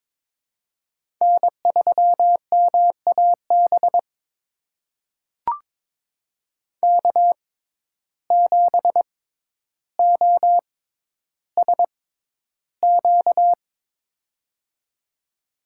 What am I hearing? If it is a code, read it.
Morse code, heard as a keyed tone, about 22 wpm, spaced as for 11 wpm, 700 Hz. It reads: N3MAB K7OSQ
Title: USA Call Signs - 5 Characters - 6x Character Spacing - 22wpm